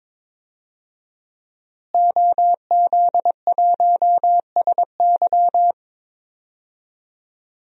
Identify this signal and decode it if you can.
Morse code, heard as a keyed tone, about 22 wpm, 700 Hz. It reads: OZ1SY
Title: DXCC-100 Call Signs - 5 Characters - Repeated 2x - 22wpm